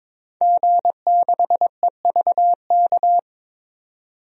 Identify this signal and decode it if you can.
Morse code, heard as a keyed tone, about 21 wpm, 700 Hz. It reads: G6EVK
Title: DXCC-100 Call Signs - 5 Characters - 22wpm